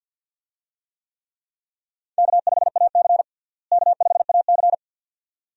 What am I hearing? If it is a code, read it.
Morse code, heard as a keyed tone, about 52 wpm, 700 Hz. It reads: K5AC K5AC
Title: USA Call Signs - 4 Characters - Repeated 2x - 50wpm